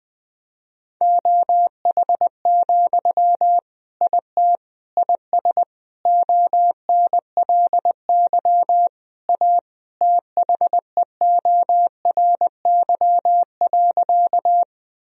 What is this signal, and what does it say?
Morse code, heard as a keyed tone, about 20 wpm, 700 Hz. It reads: OH, IT IS ONLY A THEORY.
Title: Sentences from Top 1000 Words - Review Part 3 - 20wpm